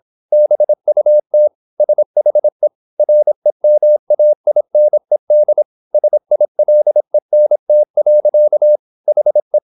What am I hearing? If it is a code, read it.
Morse code, heard as a keyed tone, about 26 wpm, 600 Hz. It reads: BUT SHE REMAINED SILENT. HE
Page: Never 26wpm | CW med Gnister